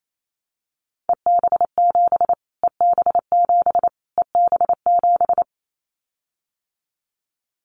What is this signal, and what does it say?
Morse code, heard as a keyed tone, about 28 wpm, 700 Hz. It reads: EB7 EB7 EB7
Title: Callsign Prefixes for Top 25 DXCC - Repeated 3x - 28wpm